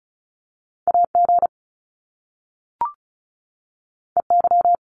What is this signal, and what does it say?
Morse code, heard as a keyed tone, about 36 wpm, 700 Hz. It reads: AG EY
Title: Top Two Letter Combinations in Words 35wpm